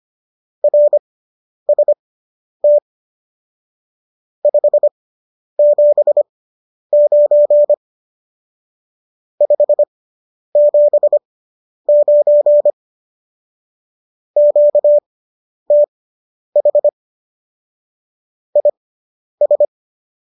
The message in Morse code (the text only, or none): RST 579 579 QTH IS